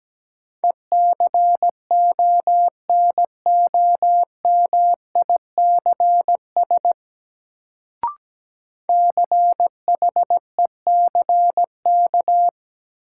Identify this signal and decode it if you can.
Morse code, heard as a keyed tone, about 17 wpm, 700 Hz. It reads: ECONOMICS CHECK
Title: Sentences from Top 1000 Words - Review Part 3 - New Word Drill - 17wpm